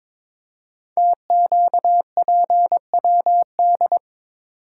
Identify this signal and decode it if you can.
Morse code, heard as a keyed tone, about 22 wpm, 700 Hz. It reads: TQPWD
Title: Code Groups - 5 Letters - 22wpm